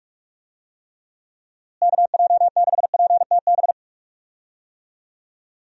Morse code, heard as a keyed tone, about 46 wpm, 700 Hz. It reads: KJ6PTB
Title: USA Call Signs - 6 Characters - Repeated 3x - 45wpm